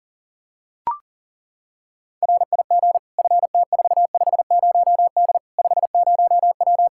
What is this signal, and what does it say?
Morse code, heard as a keyed tone, about 40 wpm, 700 Hz. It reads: RIG FT450D 50W
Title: Practice QSO - Exchange 2 - Info - 40wpm